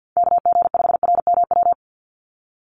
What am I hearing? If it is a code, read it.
Morse code, heard as a keyed tone, about 50 wpm, 700 Hz. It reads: KG5RMW
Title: USA Call Signs 50wpm